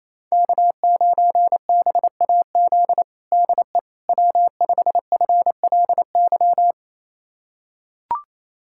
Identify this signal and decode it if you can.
Morse code, heard as a keyed tone, about 27 wpm, 700 Hz. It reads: K9BAZ DE W5FLY
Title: USA Callsign DE Callsign - 5 Characters - 28wpm